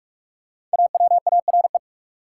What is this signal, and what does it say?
Morse code, heard as a keyed tone, about 41 wpm, 700 Hz. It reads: AWARE
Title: Sentences from Top 1200 Words - New Word Drill Alt - 45wpm